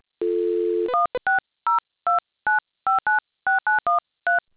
DTMF, heard as telephone touch tones, tones of about 140 ms. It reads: 16029596913